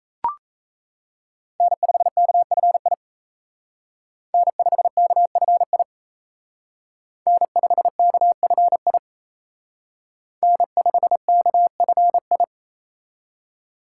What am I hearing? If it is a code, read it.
Morse code, heard as a keyed tone, about 33 wpm, 700 Hz. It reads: N5KFI N5KFI N5KFI N5KFI
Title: Speed Racing - USA 1x3 Call Signs 28wpm